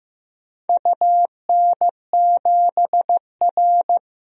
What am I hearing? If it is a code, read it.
Morse code, heard as a keyed tone, about 15 wpm, 700 Hz. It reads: UN7R
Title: DXCC-100 Call Signs - 4 Characters - 15wpm